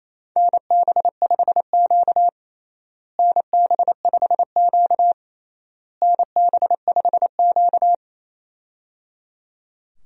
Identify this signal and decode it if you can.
Morse code, heard as a keyed tone, about 28 wpm, 700 Hz. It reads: NB5Q NB5Q NB5Q
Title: USA Call Signs - 4 Characters - Repeated 3x - 28wpm